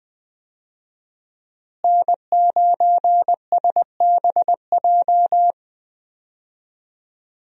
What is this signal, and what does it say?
Morse code, heard as a keyed tone, about 20 wpm, 700 Hz. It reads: N9SBJ
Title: USA Call Signs - 5 Characters - Repeated 3x - 20wpm